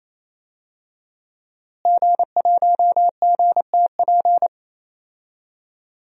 Morse code, heard as a keyed tone, about 28 wpm, 700 Hz. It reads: G1GTP
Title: DXCC-100 Call Signs - 5 Characters - Repeated 3x - 28wpm